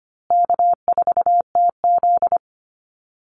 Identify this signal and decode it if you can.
Morse code, heard as a keyed tone, about 25 wpm, 700 Hz. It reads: K4TZ
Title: Easy Call Signs Repeated 3x - 25wpm